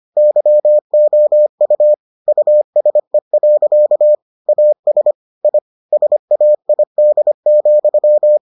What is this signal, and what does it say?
Morse code, heard as a keyed tone, about 25 wpm, 600 Hz. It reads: YOU USE. AS I SAID,